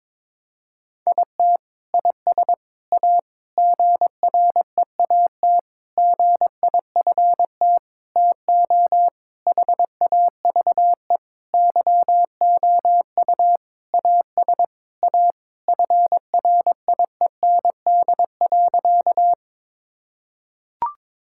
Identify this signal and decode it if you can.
Morse code, heard as a keyed tone, about 22 wpm, 700 Hz. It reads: IT IS A GREAT GIFT TO HAVE YOU AS A FRIEND.
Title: Sentences from Top 1000 Words - Review Part 6 - 22wpm